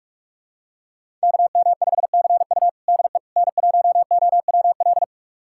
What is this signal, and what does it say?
Morse code, heard as a keyed tone, about 41 wpm, 700 Hz. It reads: KM5CU DE N1OWL
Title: USA Callsign DE Callsign - 5 Characters - 45wpm